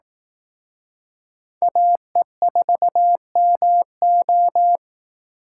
Morse code, heard as a keyed tone, about 18 wpm, 700 Hz. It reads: AE4MO
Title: Speed Racing - USA 2x2 Call Signs 15wpm